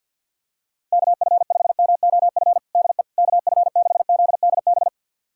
Morse code, heard as a keyed tone, about 53 wpm, 700 Hz. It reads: KF5MOF DE KF6ZDB
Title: USA Callsign DE Callsign - 6 Characters - 50wpm